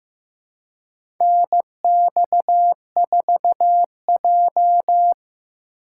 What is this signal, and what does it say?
Morse code, heard as a keyed tone, about 15 wpm, 700 Hz. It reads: NX4J
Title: USA Call Signs - 4 Characters - 4x Character Spacing - 15wpm